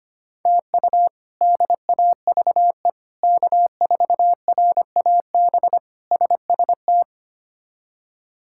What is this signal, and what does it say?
Morse code, heard as a keyed tone, about 25 wpm, 700 Hz. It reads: TU DAVE K4RAB SST
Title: K1USN SST QSO Practice - 25wpm